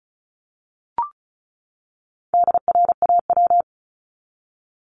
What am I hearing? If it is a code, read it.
Morse code, heard as a keyed tone, about 34 wpm, 700 Hz. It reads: DRAW
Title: ING Suffix - Top 100 Words with Base Words 35wpm